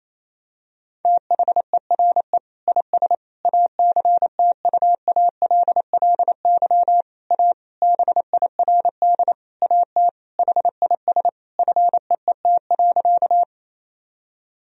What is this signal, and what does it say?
Morse code, heard as a keyed tone, about 28 wpm, 700 Hz. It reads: THERE IS ACTUALLY A BIRD AT HIS FEET.
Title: Sentences from Top 1000 Words - Review Part 4 - 28wpm